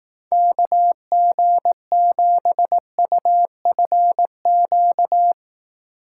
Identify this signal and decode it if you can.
Morse code, heard as a keyed tone, about 18 wpm, 700 Hz. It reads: KG7UFQ
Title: Speed Racing - USA 2x3 Call Signs 15wpm